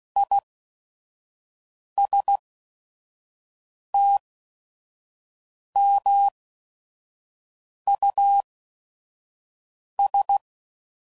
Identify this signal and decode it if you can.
Morse code, heard as a keyed tone, about 16 wpm, spaced as for 5 wpm, 800 Hz. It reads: ISTMUS
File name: Prefixes / By First Letter / Is